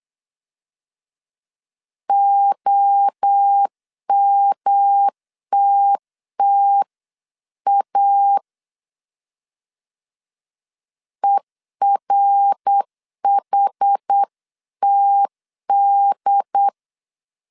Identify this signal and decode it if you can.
Morse code, heard as a keyed tone, about 8 wpm, 800 Hz. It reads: OMTTA ERHTD